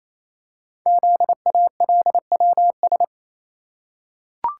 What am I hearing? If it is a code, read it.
Morse code, heard as a keyed tone, about 29 wpm, 700 Hz. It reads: ZALWS
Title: Code Groups - 5 Letters - 28wpm